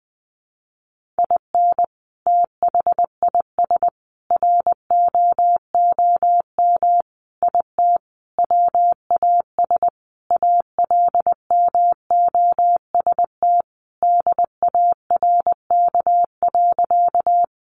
Morse code, heard as a keyed tone, about 20 wpm, 700 Hz. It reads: IN THIS ROOM IT WAS ALMOST DARK.
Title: Sentences from Top 700 Words 20wpm